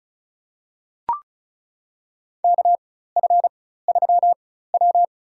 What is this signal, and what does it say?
Morse code, heard as a keyed tone, about 36 wpm, spaced as for 16 wpm, 700 Hz. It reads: KF3W
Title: USA Call Signs - 4 Characters - 4x Character Spacing - 35wpm